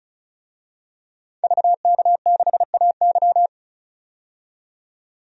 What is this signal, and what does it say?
Morse code, heard as a keyed tone, about 34 wpm, 700 Hz. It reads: VK6AY
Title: DXCC-100 Call Signs - 5 Characters - 35wpm